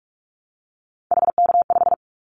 Morse code, heard as a keyed tone, about 44 wpm, 700 Hz. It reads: HK5
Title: Callsign Prefixes for Top 100 DXCC - 45wpm